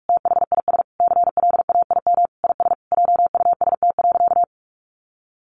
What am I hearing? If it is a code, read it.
Morse code, heard as a keyed tone, about 44 wpm, 700 Hz. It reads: THIS CLAIM IS JUST.
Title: Sentences from Top 1500 Words 45wpm